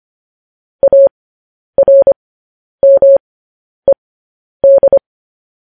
Morse code, heard as a keyed tone, about 25 wpm, spaced as for 10 wpm, 550 Hz. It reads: ARMED
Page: news headlines in morse code at 10 wpm